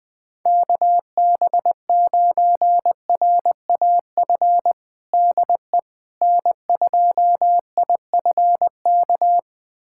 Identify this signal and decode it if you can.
Morse code, heard as a keyed tone, about 20 wpm, 700 Hz. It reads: KB9RAF DE N2IFK
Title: USA Callsign DE Callsign 20wpm